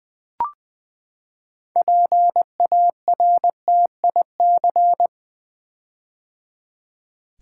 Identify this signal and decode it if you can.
Morse code, heard as a keyed tone, about 20 wpm, 700 Hz. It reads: PARTIC
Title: Build A Word - 6 Letters At A Time - 12-Letter Words - 20wpm